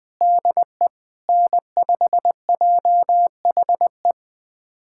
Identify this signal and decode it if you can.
Morse code, heard as a keyed tone, about 21 wpm, 700 Hz. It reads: DE N5JHE
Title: USA Callsign DE Callsign - 5 Characters - 20wpm